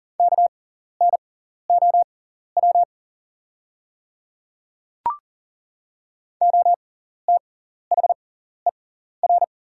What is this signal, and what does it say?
Morse code, heard as a keyed tone, about 40 wpm, spaced as for 14 wpm, 690 Hz. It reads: KNOW OTHER
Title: 6x Character Spacing - Top 200 Words - 40wpm